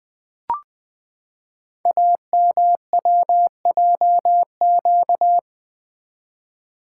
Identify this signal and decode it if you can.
Morse code, heard as a keyed tone, about 20 wpm, 700 Hz. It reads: AMWJQ
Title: Code Groups - 5 Letters - 20wpm